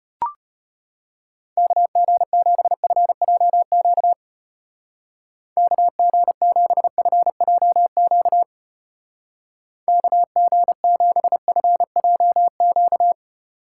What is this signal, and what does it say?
Morse code, heard as a keyed tone, about 31 wpm, 700 Hz. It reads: KG7FJQ KG7FJQ KG7FJQ
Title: Speed Racing - USA Call Signs 25wpm